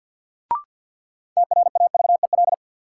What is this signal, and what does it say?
Morse code, heard as a keyed tone, about 50 wpm, 700 Hz. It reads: TRAVEL
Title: Sentences from Top 400 Words - New Word Drill - 50wpm